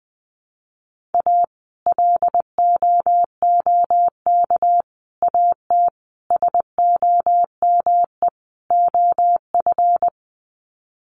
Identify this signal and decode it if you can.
Morse code, heard as a keyed tone, about 20 wpm, 700 Hz. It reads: A LOOK AT SOME OF
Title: Sets of 5 Words Encore2 20wpm